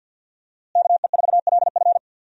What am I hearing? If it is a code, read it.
Morse code, heard as a keyed tone, about 53 wpm, 700 Hz. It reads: KE4LF
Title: USA Call Signs - 5 Characters - Repeated 2x - 50wpm